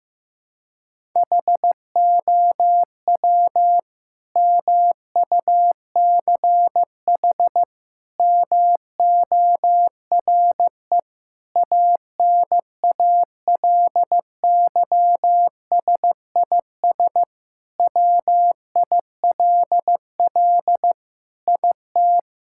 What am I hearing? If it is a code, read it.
Morse code, heard as a keyed tone, about 15 wpm, 700 Hz. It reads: HOW MUCH MORE ANALYSIS WILL IT
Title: Sentences from Top 1000 Words - Review Part 5 - 15wpm